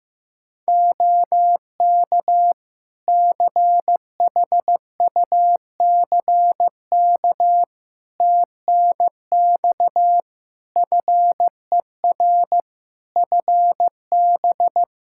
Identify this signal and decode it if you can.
Morse code, heard as a keyed tone, about 15 wpm, 700 Hz. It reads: OK CHUCK TNX FER FB
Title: Practice QSO - Exchange 4 - Ending - 15wpm